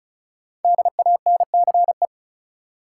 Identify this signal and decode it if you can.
Morse code, heard as a keyed tone, about 34 wpm, 700 Hz. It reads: DANCE
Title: Sentences from Top 700 Words - New Word Drill - 35wpm